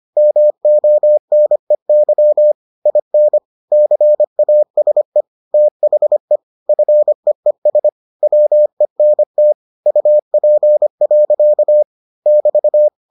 Morse code, heard as a keyed tone, about 25 wpm, 600 Hz. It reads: MONEY IN CASE THE FEES WENT UP. <BT>